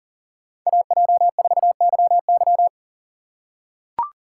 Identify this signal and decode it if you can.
Morse code, heard as a keyed tone, about 40 wpm, 700 Hz. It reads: AJ4YY
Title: USA Call Signs - 5 Characters - 3x Character Spacing - 40wpm